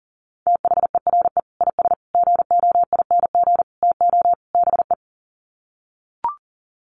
Morse code, heard as a keyed tone, about 40 wpm, 700 Hz. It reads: THERE IS GOING TO BE
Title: Sets of 5 Words Encore 40wpm